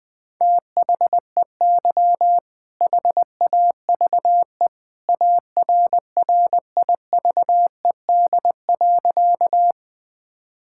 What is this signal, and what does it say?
Morse code, heard as a keyed tone, about 21 wpm, 700 Hz. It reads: THEY HAVE ARRIVED.
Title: Sentences with 2-Word Phrases - 20wpm